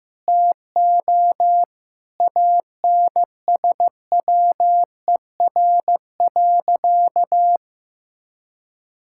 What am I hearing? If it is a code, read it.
Morse code, heard as a keyed tone, about 15 wpm, 700 Hz. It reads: TO ANSWER.
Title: Sentences from Top 1600 Words 15wpm